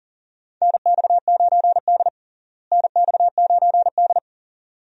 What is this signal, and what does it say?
Morse code, heard as a keyed tone, about 40 wpm, 700 Hz. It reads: NX9D NX9D
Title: USA Call Signs - 4 Characters - Repeated 2x - 40wpm